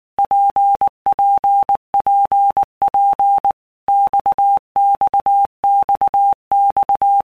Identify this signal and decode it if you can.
Morse code, heard as a keyed tone, about 19 wpm, 800 Hz. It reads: PPPP XXXX
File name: All Groups 1-minute Blocks / Groups@18 / RKPX